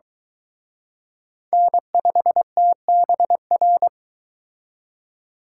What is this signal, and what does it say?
Morse code, heard as a keyed tone, about 22 wpm, 700 Hz. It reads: N5TBR